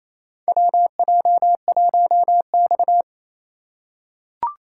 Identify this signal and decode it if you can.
Morse code, heard as a keyed tone, about 28 wpm, 700 Hz. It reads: WJ1X
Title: USA Call Signs - 4 Characters - 2x Character Spacing - 28wpm